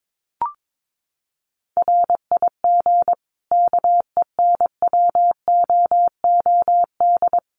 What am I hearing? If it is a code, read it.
Morse code, heard as a keyed tone, about 22 wpm, 700 Hz. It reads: RIG KENWOOD
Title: Common QSO Phrases 22wpm